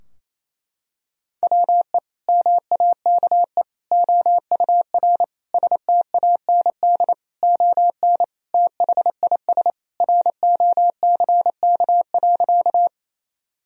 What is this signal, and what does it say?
Morse code, heard as a keyed tone, about 28 wpm, 700 Hz. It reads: WE MAKE OUR STAND ON THIS ROCK.
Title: Sentences from Top 1000 Words - Review Part 2 - 28wpm